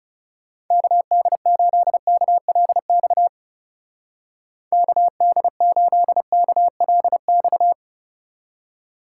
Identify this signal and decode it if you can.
Morse code, heard as a keyed tone, about 33 wpm, 700 Hz. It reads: KD8KLX KD8KLX
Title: Speed Racing - USA 2x3 Call Signs 30wpm